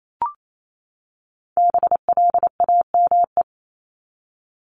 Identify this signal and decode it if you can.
Morse code, heard as a keyed tone, about 28 wpm, 700 Hz. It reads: BLAME
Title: Top 1751-2000 Words 28wpm